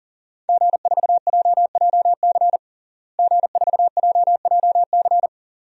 Morse code, heard as a keyed tone, about 40 wpm, 700 Hz. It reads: G4JJC G4JJC
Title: DXCC-100 Call Signs - 5 Characters - Repeated 2x - 40wpm